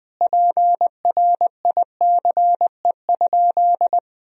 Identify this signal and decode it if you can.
Morse code, heard as a keyed tone, about 21 wpm, 700 Hz. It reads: PRICE?
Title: Sentences from Top 1000 Words - Review Part 4 - 20wpm